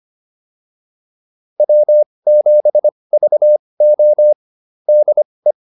Morse code, heard as a keyed tone, about 25 wpm, spaced as for 20 wpm, 600 Hz. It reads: W7VO DE
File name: QSO105a 20